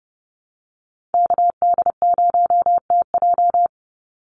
Morse code, heard as a keyed tone, about 30 wpm, 700 Hz. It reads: KD0TJ
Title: USA Call Signs 30wpm